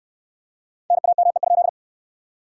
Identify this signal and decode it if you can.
Morse code, heard as a keyed tone, about 63 wpm, 700 Hz. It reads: NAME?